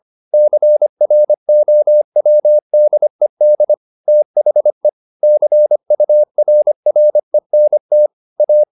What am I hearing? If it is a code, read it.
Morse code, heard as a keyed tone, about 25 wpm, 600 Hz. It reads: CROWDED THE CURRENT A
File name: the old man and the sea part 1 25wpm